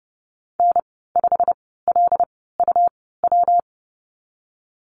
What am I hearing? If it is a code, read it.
Morse code, heard as a keyed tone, about 30 wpm, spaced as for 17 wpm, 700 Hz. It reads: N5LUW
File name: Lesson 19g - Letter w - Call Signs 30wpm